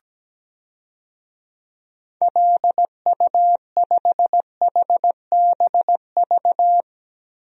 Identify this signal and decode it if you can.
Morse code, heard as a keyed tone, about 17 wpm, 700 Hz. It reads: LU5HBV
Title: DXCC-100 Call Signs - 6 Characters - Repeated 2x - 17wpm